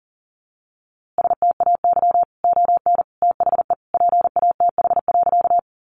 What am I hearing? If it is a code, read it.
Morse code, heard as a keyed tone, about 40 wpm, 700 Hz. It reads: STAY ON THE PATH.
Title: A Sound Of Thunder 40wpm